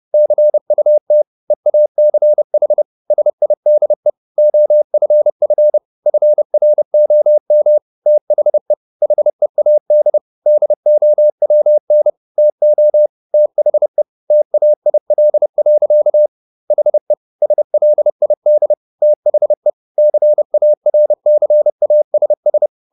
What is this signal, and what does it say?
Morse code, heard as a keyed tone, about 30 wpm, 600 Hz. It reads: CUT EACH SIDE OFF FROM THE HEAD DOWN TO THE TAIL. HE SLID THE CARCASS